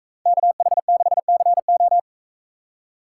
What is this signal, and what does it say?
Morse code, heard as a keyed tone, about 44 wpm, 700 Hz. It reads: KH6CO